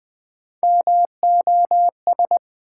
Morse code, heard as a keyed tone, about 20 wpm, 700 Hz. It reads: MOS
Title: Build A Word - 3 Letters At A Time - 6-Letter Words - 20wpm